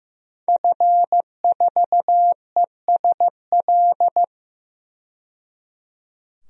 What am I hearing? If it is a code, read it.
Morse code, heard as a keyed tone, about 15 wpm, 700 Hz. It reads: F4ESL
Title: DXCC-100 Call Signs - 5 Characters - 15wpm